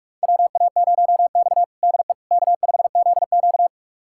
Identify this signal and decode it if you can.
Morse code, heard as a keyed tone, about 44 wpm, 690 Hz. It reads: WA0X DE K5ZQ
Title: USA Callsign DE Callsign - 4 Characters - 45wpm